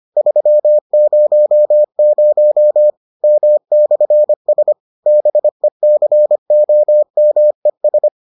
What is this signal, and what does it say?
Morse code, heard as a keyed tone, about 25 wpm, 600 Hz. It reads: 300 M/S BECOMES